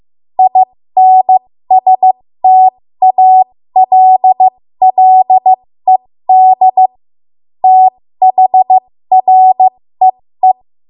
Morse code, heard as a keyed tone, about 15 wpm, spaced as for 13 wpm, 750 Hz. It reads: INSTALLED THREE
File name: Morse / 13WPM